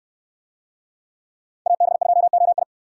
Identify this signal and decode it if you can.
Morse code, heard as a keyed tone, about 64 wpm, 700 Hz. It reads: AB2OI